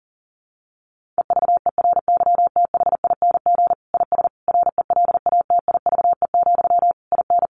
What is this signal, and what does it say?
Morse code, heard as a keyed tone, about 40 wpm, 700 Hz. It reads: EVERYTHING IS RELATIVE, IN
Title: Sentences from Top 1400 Words 40wpm